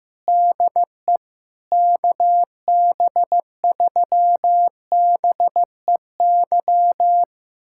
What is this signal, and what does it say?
Morse code, heard as a keyed tone, about 15 wpm, 700 Hz. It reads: DE KB3BEY
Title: USA Callsign DE Callsign - 6 Characters - 15wpm